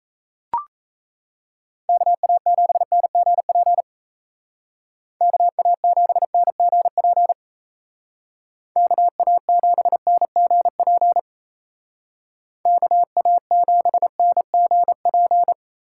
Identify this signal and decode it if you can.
Morse code, heard as a keyed tone, about 34 wpm, 700 Hz. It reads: KA7NGP KA7NGP KA7NGP KA7NGP
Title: Speed Racing - USA 2x3 Call Signs 28wpm